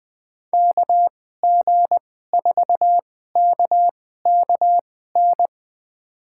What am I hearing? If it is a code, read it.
Morse code, heard as a keyed tone, about 20 wpm, spaced as for 14 wpm, 700 Hz. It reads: KG4KKN
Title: USA Call Signs - 6 Characters - 2x Character Spacing - 20wpm